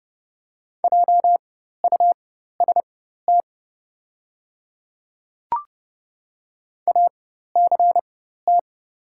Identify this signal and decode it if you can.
Morse code, heard as a keyed tone, about 30 wpm, spaced as for 14 wpm, 700 Hz. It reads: JUST ACT
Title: 4x Character Spacing - Top 200 Words - 30wpm